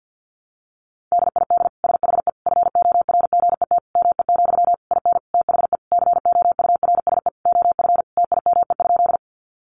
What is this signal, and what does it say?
Morse code, heard as a keyed tone, about 50 wpm, 700 Hz. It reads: DID SHE FORGET ME, IN THE COURSE OF TIME?
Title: Sentences from Top 1200 Words 50wpm